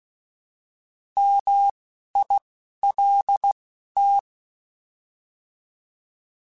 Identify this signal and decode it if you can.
Morse code, heard as a keyed tone, about 16 wpm, spaced as for 11 wpm, 800 Hz. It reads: MILT
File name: Alphabetical / Alphabetical mp3s / Ms @10